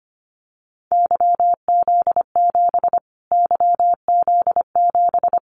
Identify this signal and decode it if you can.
Morse code, heard as a keyed tone, about 25 wpm, 700 Hz. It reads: YZ7 YZ7
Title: Callsign Prefixes for Top 25 DXCC - Repeated 3x - 25wpm